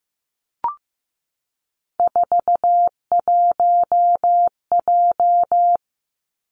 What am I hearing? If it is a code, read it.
Morse code, heard as a keyed tone, about 15 wpm, 700 Hz. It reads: 41J
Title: Three Challenging Letter-Number Combinations 15wpm